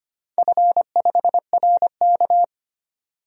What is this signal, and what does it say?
Morse code, heard as a keyed tone, about 25 wpm, 700 Hz. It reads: F5RK
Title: DXCC-100 Call Signs - 4 Characters - 25wpm